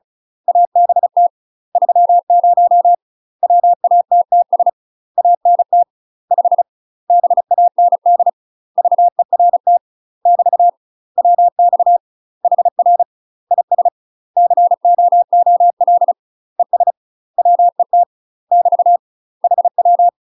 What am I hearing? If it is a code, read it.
Morse code, heard as a keyed tone, about 36 wpm, 700 Hz. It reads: ABT 30 WATTS ANT 5 BAND VERT <BT> WX HR IS COOL ES WET <BT> HW
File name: UK MASTER QSOs TRAINING / 35WPM / UK MASTER QSOs 0031